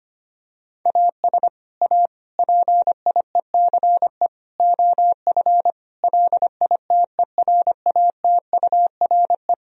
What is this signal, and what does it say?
Morse code, heard as a keyed tone, about 25 wpm, 700 Hz. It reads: AS A PIECE OF LITERATURE
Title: Sentences from Top 1000 Words - Review Part 3 - 25wpm